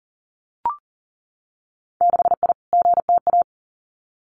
Morse code, heard as a keyed tone, about 40 wpm, 700 Hz. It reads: 6I GTA
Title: Winter Field Day Exchanges Encore 40wpm